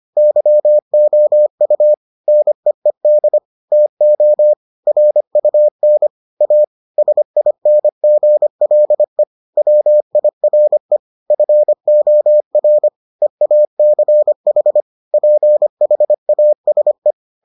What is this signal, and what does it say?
Morse code, heard as a keyed tone, about 25 wpm, 600 Hz. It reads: YOU NEED TO RUN A SINGLE WIRE FOR EACH PHASE